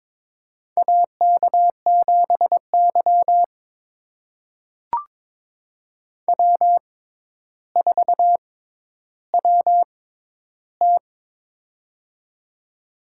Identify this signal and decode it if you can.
Morse code, heard as a keyed tone, about 21 wpm, spaced as for 11 wpm, 700 Hz. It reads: AK7Y W4WT
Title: USA Call Signs - 4 Characters - 6x Character Spacing - 22wpm